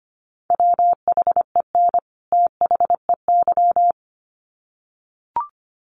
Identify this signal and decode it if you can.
Morse code, heard as a keyed tone, about 25 wpm, 700 Hz. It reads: WHEN THEY
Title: Sets of 2 Words 25wpm